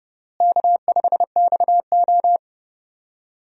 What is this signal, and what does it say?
Morse code, heard as a keyed tone, about 32 wpm, 700 Hz. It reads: K5XO
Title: USA Call Signs - 4 Characters - Repeated 2x - 30wpm